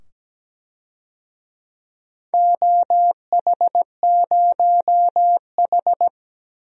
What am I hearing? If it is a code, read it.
Morse code, heard as a keyed tone, about 17 wpm, 700 Hz. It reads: OH0H
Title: DXCC-100 Call Signs - 4 Characters - 17wpm